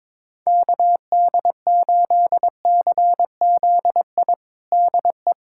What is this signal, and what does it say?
Morse code, heard as a keyed tone, about 21 wpm, 700 Hz. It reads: KD8CZI DE
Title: USA Callsign DE Callsign 22wpm